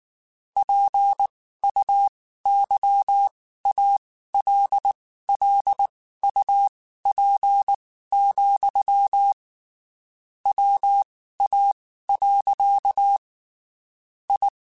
Morse code, heard as a keyed tone, about 19 wpm, spaced as for 14 wpm, 800 Hz. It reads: PUYALLUP, WA. I
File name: (spaced) / QSOs Spaced-mp3s / kc7haa @18 sp